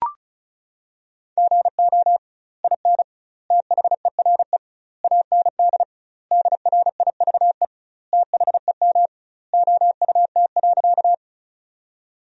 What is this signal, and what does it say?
Morse code, heard as a keyed tone, about 34 wpm, 700 Hz. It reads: GO IN THERE AND DRIVE THEM OUT.
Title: 2x Word-Spacing - Sentences from Top 500 Words - 35wpm